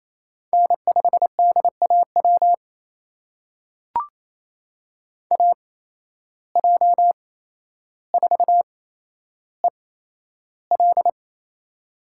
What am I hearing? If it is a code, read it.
Morse code, heard as a keyed tone, about 28 wpm, spaced as for 12 wpm, 700 Hz. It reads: N5DAW AJ4EL
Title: USA Call Signs - 5 Characters - 8x Character Spacing - 28wpm